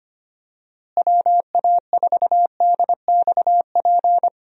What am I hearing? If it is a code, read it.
Morse code, heard as a keyed tone, about 25 wpm, 700 Hz. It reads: WA4DXP
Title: USA Call Signs - 6 Characters - Repeated 3x - 25wpm